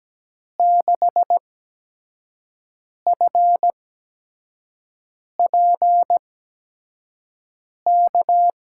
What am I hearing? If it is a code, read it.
Morse code, heard as a keyed tone, about 17 wpm, spaced as for 5 wpm, 700 Hz. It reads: 6FPK
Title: USA Call Signs - 6 Characters - 8x Character Spacing - 17wpm